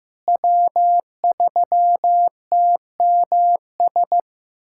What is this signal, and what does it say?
Morse code, heard as a keyed tone, about 15 wpm, 700 Hz. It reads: W3TMS
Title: USA Callsign DE Callsign - 5 Characters - 15wpm